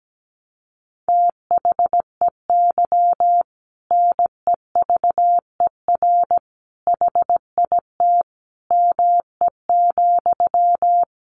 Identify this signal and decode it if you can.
Morse code, heard as a keyed tone, about 17 wpm, 700 Hz. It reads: THEY NEVER HIT ME,
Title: Sentences from Top 1200 Words 17wpm